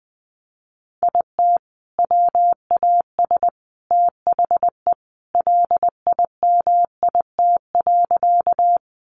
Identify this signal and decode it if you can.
Morse code, heard as a keyed tone, about 20 wpm, 700 Hz. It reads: IT WAS THE LIMIT.
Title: Sentences from Top 1500 Words 20wpm